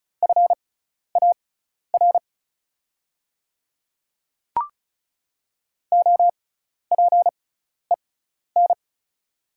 Morse code, heard as a keyed tone, about 36 wpm, spaced as for 12 wpm, 690 Hz. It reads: FAR OPEN
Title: 6x Character Spacing - Top 300 Words - 35wpm